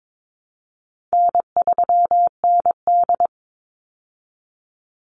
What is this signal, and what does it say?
Morse code, heard as a keyed tone, about 22 wpm, 700 Hz. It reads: N3ND
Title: Easy Call Signs Repeated 3x - 22wpm